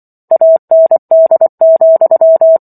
Morse code, heard as a keyed tone, about 24 wpm, 650 Hz. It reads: AND,